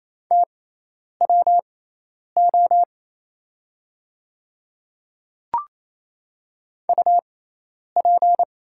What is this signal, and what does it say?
Morse code, heard as a keyed tone, about 27 wpm, spaced as for 10 wpm, 700 Hz. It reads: TWO UP
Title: 6x Character Spacing - Top 200 Words - 28wpm